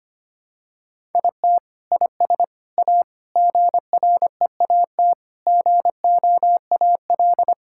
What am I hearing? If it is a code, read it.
Morse code, heard as a keyed tone, about 25 wpm, 700 Hz. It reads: IT IS A GREAT GOAL
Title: Sentences from Top 1000 Words - Review Part 5 - 25wpm